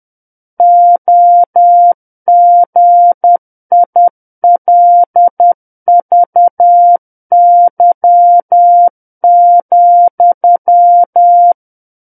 Morse code, heard as a keyed tone, about 10 wpm, 700 Hz. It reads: OGILVY,
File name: Book One / 01-Chapter 1 - 10 WPM